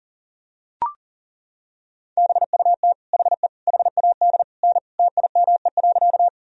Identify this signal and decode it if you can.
Morse code, heard as a keyed tone, about 40 wpm, 700 Hz. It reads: BUT HE HAD N TIME.
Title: Sentences from Top 100 Words and Abbreviations 40wpm